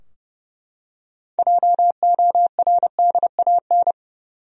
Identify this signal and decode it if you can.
Morse code, heard as a keyed tone, about 30 wpm, 700 Hz. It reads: JORDAN
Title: USA Common Names - 30wpm